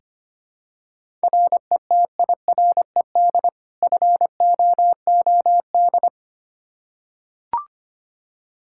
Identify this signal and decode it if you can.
Morse code, heard as a keyed tone, about 25 wpm, 700 Hz. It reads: RETIRED FOOD
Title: Practice QSO - Exchange 3 - More Info - 25wpm